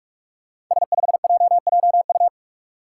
Morse code, heard as a keyed tone, about 44 wpm, 700 Hz. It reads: S5JJU
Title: Five Challenging Letter-Number Combinations 45wpm